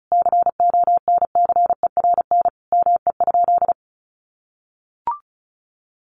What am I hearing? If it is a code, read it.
Morse code, heard as a keyed tone, about 34 wpm, 700 Hz. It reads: CONCERN ME?
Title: Sentences from Top 1300 Words 35wpm